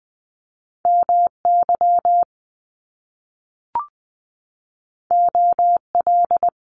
Morse code, heard as a keyed tone, about 20 wpm, 700 Hz. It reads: M Y O L
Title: Top Two Letter Combinations in Words 20wpm